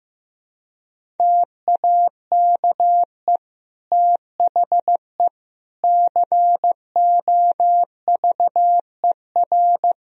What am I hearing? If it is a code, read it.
Morse code, heard as a keyed tone, about 16 wpm, 700 Hz. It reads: TAKE THE COVER
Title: Sentences from Top 1000 Words - Review Part 1 - 15wpm